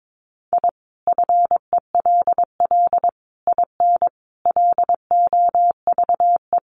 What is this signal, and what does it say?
Morse code, heard as a keyed tone, about 22 wpm, 700 Hz. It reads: I FELL IN LOVE
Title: Sets of 5 Words Encore 22wpm